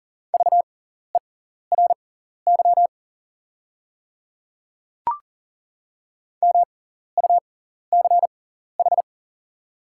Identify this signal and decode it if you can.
Morse code, heard as a keyed tone, about 38 wpm, spaced as for 14 wpm, 690 Hz. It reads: VERY MUCH
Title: 6x Character Spacing - Top 200 Words - 40wpm